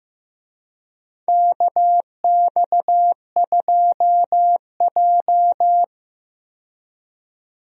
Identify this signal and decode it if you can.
Morse code, heard as a keyed tone, about 15 wpm, 700 Hz. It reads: KX2J
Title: USA Call Signs - 4 Characters - 15wpm